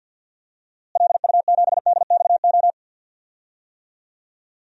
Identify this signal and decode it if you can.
Morse code, heard as a keyed tone, about 52 wpm, 700 Hz. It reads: LU7DXO
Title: DXCC-100 Call Signs - 6 Characters - Repeated 2x - 50wpm